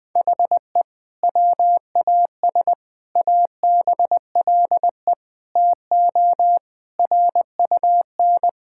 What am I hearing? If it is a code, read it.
Morse code, heard as a keyed tone, about 21 wpm, 700 Hz. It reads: HE WAS ABLE TO RUN
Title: Sentences with 2-Word Phrases - 20wpm